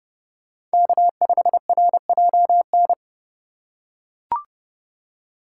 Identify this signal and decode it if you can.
Morse code, heard as a keyed tone, about 31 wpm, 700 Hz. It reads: K5RJN
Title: USA Call Signs - 5 Characters - 2x Character Spacing - 30wpm